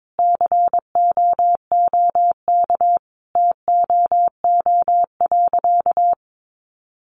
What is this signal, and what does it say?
Morse code, heard as a keyed tone, about 22 wpm, 700 Hz. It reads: COOK TOO.
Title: Sentences from Top 700 Words 22wpm